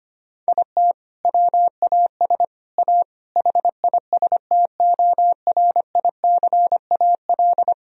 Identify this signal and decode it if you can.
Morse code, heard as a keyed tone, about 25 wpm, 700 Hz. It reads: IT WAS A HISTORICAL
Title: Sentences from Top 1000 Words - Review Part 4 - 25wpm